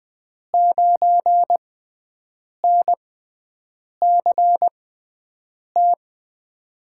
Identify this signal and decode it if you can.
Morse code, heard as a keyed tone, about 21 wpm, spaced as for 7 wpm, 700 Hz. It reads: 9NCT
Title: USA Call Signs - 6 Characters - 6x Character Spacing - 20wpm